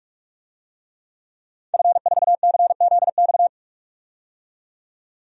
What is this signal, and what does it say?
Morse code, heard as a keyed tone, about 48 wpm, 700 Hz. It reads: F4CZX
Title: DXCC-100 Call Signs - 5 Characters - 45wpm